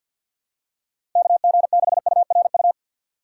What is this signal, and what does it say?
Morse code, heard as a keyed tone, about 53 wpm, 700 Hz. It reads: KG6URU
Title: USA Call Signs - 2x Character Spacing - 50wpm